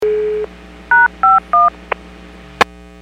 DTMF, heard as telephone touch tones, tones of about 180 ms. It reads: #51
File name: WPS / 02397 06-47 WPS 451.25 WPS